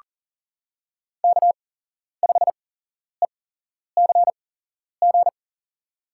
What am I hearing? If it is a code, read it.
Morse code, heard as a keyed tone, about 40 wpm, spaced as for 11 wpm, 700 Hz. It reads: K5ECG
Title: USA Call Signs - 5 Characters - 8x Character Spacing - 40wpm